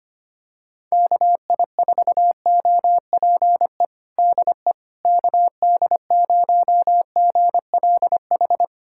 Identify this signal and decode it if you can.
Morse code, heard as a keyed tone, about 25 wpm, 700 Hz. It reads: KI4OPE DE KD0GLH
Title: USA Callsign DE Callsign 25wpm